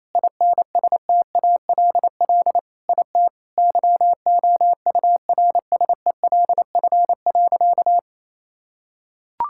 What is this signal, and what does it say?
Morse code, heard as a keyed tone, about 28 wpm, 700 Hz. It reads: INSTALL IT YOURSELF.
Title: Sentences from Top 1600 Words 28wpm